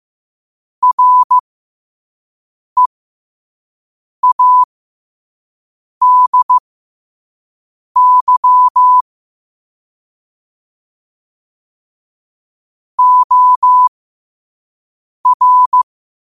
Morse code, heard as a keyed tone, about 15 wpm, spaced as for 5 wpm, 1000 Hz. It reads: READY OR
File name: qotdfiles / 5wpm